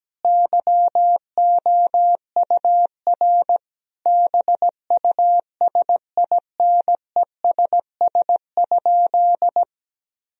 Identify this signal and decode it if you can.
Morse code, heard as a keyed tone, about 17 wpm, 700 Hz. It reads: YOUR BUSINESS?
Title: Sentences from Top 1000 Words - Review Part 4 - 17wpm